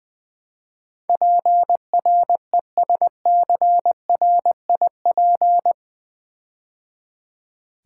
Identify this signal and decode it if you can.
Morse code, heard as a keyed tone, about 20 wpm, 700 Hz. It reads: PRESCRIP
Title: Build A Word - 4 Letters At A Time - 12-Letter Words - 20wpm